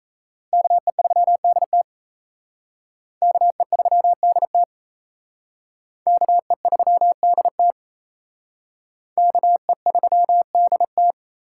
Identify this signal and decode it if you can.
Morse code, heard as a keyed tone, about 36 wpm, 700 Hz. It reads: KE3DT KE3DT KE3DT KE3DT